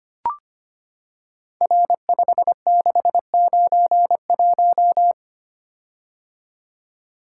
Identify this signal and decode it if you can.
Morse code, heard as a keyed tone, about 25 wpm, 700 Hz. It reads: R5691
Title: Code Groups - 5 Letter:Numbers - 25wpm